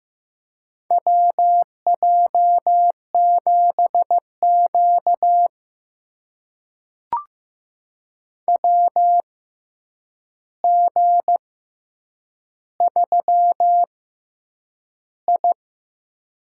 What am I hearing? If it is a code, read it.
Morse code, heard as a keyed tone, about 15 wpm, spaced as for 8 wpm, 700 Hz. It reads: WJ7Q WG3I
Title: USA Call Signs - 4 Characters - 6x Character Spacing - 15wpm